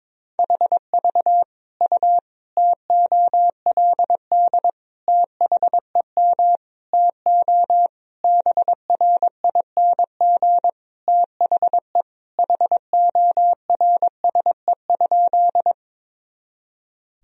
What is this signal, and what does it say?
Morse code, heard as a keyed tone, about 22 wpm, 700 Hz. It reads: HV U TOLD THEM TO BRING THE HORSE?
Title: Sentences from Top 500 Words and Abbreviations 22wpm